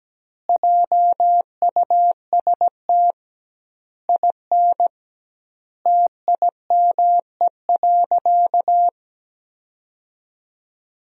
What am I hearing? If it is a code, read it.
Morse code, heard as a keyed tone, about 17 wpm, 700 Hz. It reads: JUST IN TIME.
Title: 2x Word-Spacing - Sentences from Top 300 Words - 17wpm